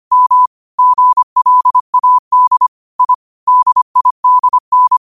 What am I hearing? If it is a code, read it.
Morse code, heard as a keyed tone, about 25 wpm, 1000 Hz. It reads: M GLAD I DIDN
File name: qotdfiles / 25wpm